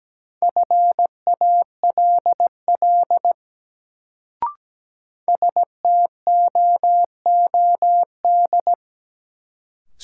Morse code, heard as a keyed tone, about 17 wpm, 700 Hz. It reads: FALL STOOD
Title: Sentences from Top 500 Words - New Word Drill - 17wpm